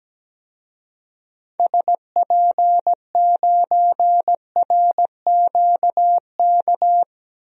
Morse code, heard as a keyed tone, about 17 wpm, 700 Hz. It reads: SP9RQK
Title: DXCC-100 Call Signs - 6 Characters - 17wpm